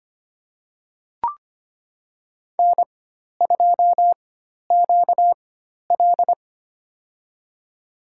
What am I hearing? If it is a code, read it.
Morse code, heard as a keyed tone, about 25 wpm, spaced as for 12 wpm, 700 Hz. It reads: N2QL
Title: USA Call Signs - 4x Character Spacing - 25wpm